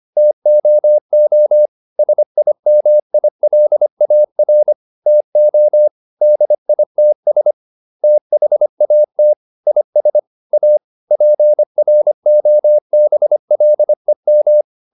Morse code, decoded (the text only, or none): TOO SIMILAR TO DITS THAT IS A PROBLEM